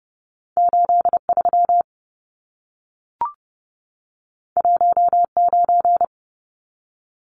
Morse code, heard as a keyed tone, about 30 wpm, 700 Hz. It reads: 83 19
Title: Numbers - 2-Digit 30wpm